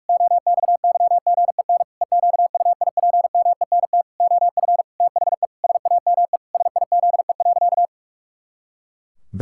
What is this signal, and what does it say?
Morse code, heard as a keyed tone, about 41 wpm, 700 Hz. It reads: OXYGEN EQUIPMENT OF THE SAME SIZE.
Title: Pangram Challenge v2 - 45wpm